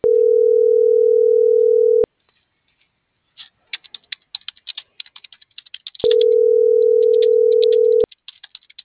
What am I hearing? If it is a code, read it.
Ringback tone (North America), 440 + 480 Hz.
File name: CENTRAL MULTIMODAL / out kimberly.suarez